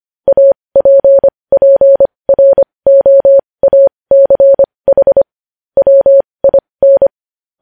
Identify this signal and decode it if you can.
Morse code, heard as a keyed tone, about 25 wpm, spaced as for 20 wpm, 550 Hz. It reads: APPROACH WIN